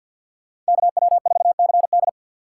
Morse code, heard as a keyed tone, about 64 wpm, 700 Hz. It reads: KW4CD